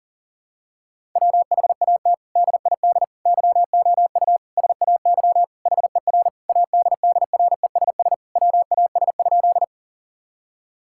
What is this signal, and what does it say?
Morse code, heard as a keyed tone, about 37 wpm, 700 Hz. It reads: WHAT DID YOU SAY HER ADDRESS WAS?
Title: Sentences from Top 1000 Words - Review Part 6 - 40wpm